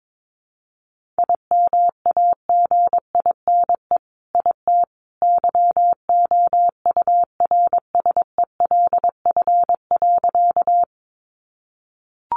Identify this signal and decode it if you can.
Morse code, heard as a keyed tone, about 22 wpm, 700 Hz. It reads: IMAGINE IT YOURSELF.
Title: Sentences from Top 1400 Words 22wpm